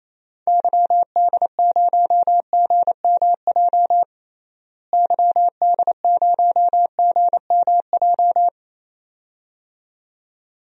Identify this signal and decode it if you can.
Morse code, heard as a keyed tone, about 29 wpm, 700 Hz. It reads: YD0GMJ YD0GMJ
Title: DXCC-100 Call Signs - 6 Characters - Repeated 2x - 28wpm